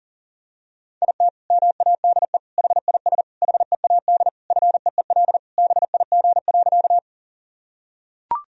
Morse code, heard as a keyed tone, about 40 wpm, 700 Hz. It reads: IT MADE HIS HEAD FEEL BIG.
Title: Sentences from Top 1000 Words - Review Part 1 - 40wpm